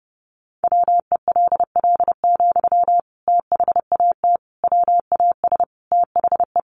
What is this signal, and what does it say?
Morse code, heard as a keyed tone, about 30 wpm, 700 Hz. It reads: WELL, THAT WAS THE
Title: Sentences from Top 1200 Words 30wpm